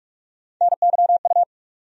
Morse code, heard as a keyed tone, about 46 wpm, 700 Hz. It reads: NYU
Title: USA Call Sign Postfixes - 3 Letters 45wpm